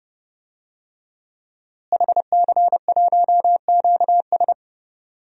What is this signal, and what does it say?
Morse code, heard as a keyed tone, about 30 wpm, 700 Hz. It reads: HC1QS
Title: DXCC-100 Call Signs - 5 Characters - Repeated 3x - 30wpm